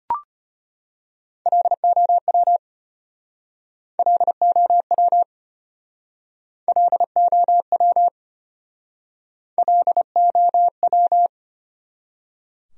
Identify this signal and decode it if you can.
Morse code, heard as a keyed tone, about 34 wpm, 700 Hz. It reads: LOW LOW LOW LOW